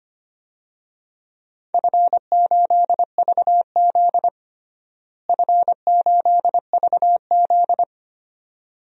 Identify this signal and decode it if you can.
Morse code, heard as a keyed tone, about 25 wpm, 700 Hz. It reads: F8VZ F8VZ
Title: DXCC-100 Call Signs - 4 Characters - Repeated 2x - 25wpm